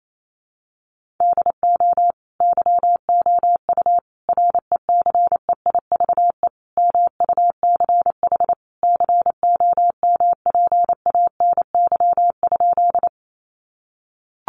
Morse code, heard as a keyed tone, about 28 wpm, 700 Hz. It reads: DO YOU RECEIVE MUCH COMPANY?
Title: Sentences from Top 1000 Words 28wpm